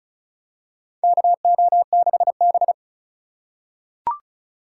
Morse code, heard as a keyed tone, about 36 wpm, 700 Hz. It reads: KO6B
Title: USA Call Signs - 2x Character Spacing - 35wpm